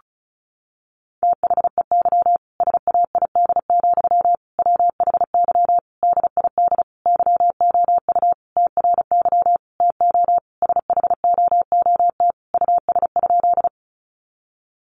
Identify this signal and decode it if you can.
Morse code, heard as a keyed tone, about 34 wpm, 700 Hz. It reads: THEY SAID, WHY DID YOU TRY TO SHOOT US?
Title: Sentences from Top 1200 Words 35wpm